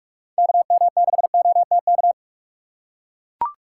Morse code, heard as a keyed tone, about 43 wpm, 700 Hz. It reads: KM6OTK
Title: USA Call Signs - 4x Character Spacing - 45wpm